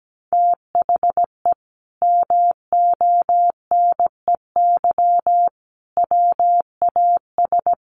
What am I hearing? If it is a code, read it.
Morse code, heard as a keyed tone, about 17 wpm, 700 Hz. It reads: THE MONEY WAS
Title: Sentences from Top 1400 Words 17wpm